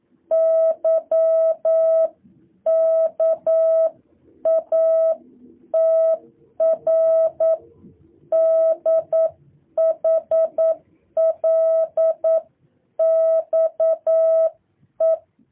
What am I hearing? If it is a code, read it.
Morse code, heard as a keyed tone, about 9 wpm, 650 Hz. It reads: YKATR DHLXE
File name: jh8cbh / 2023cw